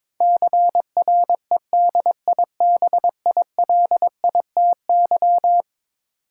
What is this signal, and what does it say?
Morse code, heard as a keyed tone, about 22 wpm, 700 Hz. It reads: CREDIBILITY
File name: practice / 22wpm / Eleven Letter Words - 22wpm